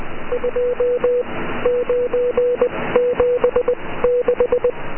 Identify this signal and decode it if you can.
Morse code, heard as a keyed tone, about 20 wpm, spaced as for 15 wpm, 490 Hz. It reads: 2976